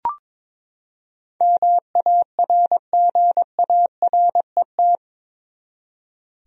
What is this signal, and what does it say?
Morse code, heard as a keyed tone, about 22 wpm, 700 Hz. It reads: MARGARET
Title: USA Common Names - 22wpm